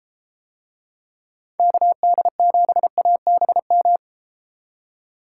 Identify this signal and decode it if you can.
Morse code, heard as a keyed tone, about 31 wpm, 700 Hz. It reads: KD7ABM